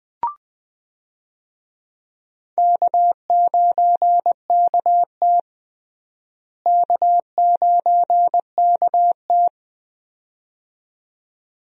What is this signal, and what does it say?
Morse code, heard as a keyed tone, about 20 wpm, 700 Hz. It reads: K9KT K9KT
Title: USA Call Signs - 4 Characters - Repeated 2x - 20wpm